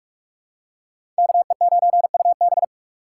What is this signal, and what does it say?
Morse code, heard as a keyed tone, about 46 wpm, 700 Hz. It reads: KE9UB
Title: USA Call Signs - 5 Characters - 3x Character Spacing - 45wpm